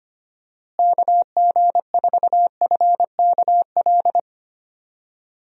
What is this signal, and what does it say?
Morse code, heard as a keyed tone, about 25 wpm, 700 Hz. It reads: KG4FKL
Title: USA Call Signs - 6 Characters - Repeated 2x - 25wpm